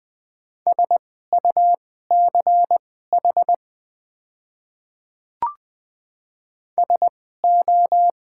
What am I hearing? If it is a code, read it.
Morse code, heard as a keyed tone, about 20 wpm, spaced as for 14 wpm, 700 Hz. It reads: SUCH SO
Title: 2x Character Spacing - Top 200 Words - 20wpm